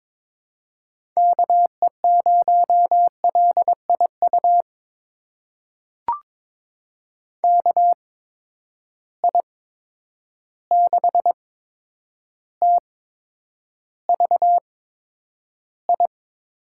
Morse code, heard as a keyed tone, about 22 wpm, spaced as for 9 wpm, 700 Hz. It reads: KE0LIU KI6TVI
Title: USA Call Signs - 8x Character Spacing - 22wpm